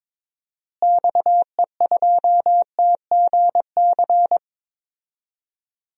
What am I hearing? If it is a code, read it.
Morse code, heard as a keyed tone, about 22 wpm, 700 Hz. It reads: XE2TGC
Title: DXCC-100 Call Signs - 6 Characters - Repeated 3x - 22wpm